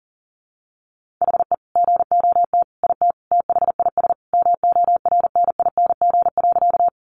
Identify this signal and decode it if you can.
Morse code, heard as a keyed tone, about 40 wpm, 700 Hz. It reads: HE GOT IT THIS MORNING.
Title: Sentences from Top 400 Words 40wpm